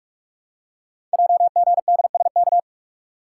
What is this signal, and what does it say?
Morse code, heard as a keyed tone, about 44 wpm, 700 Hz. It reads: JGDSK
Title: Code Groups - 5 Letter:Numbers - 45wpm